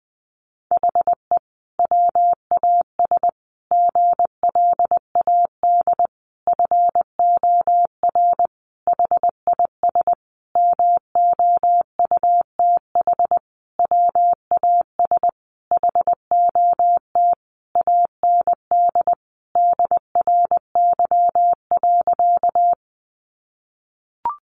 Morse code, heard as a keyed tone, about 20 wpm, 700 Hz. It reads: HE WAS GLAD FOR HIS MOUTH WAS HOT AND DRY.
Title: Sentences from Top 1500 Words 20wpm